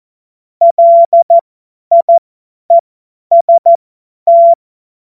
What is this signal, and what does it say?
Morse code, heard as a keyed tone, about 14 wpm, spaced as for 10 wpm, 680 Hz. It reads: LIEST
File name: Morsekurs / Witze / Witz 17. Mai 2022 50 F70